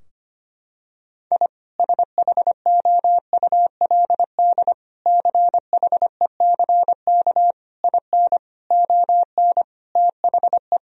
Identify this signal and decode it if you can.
Morse code, heard as a keyed tone, about 25 wpm, 700 Hz. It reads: I SHOULD CHECK IN ON THE
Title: Sentences from Top 1000 Words - Review Part 3 - 25wpm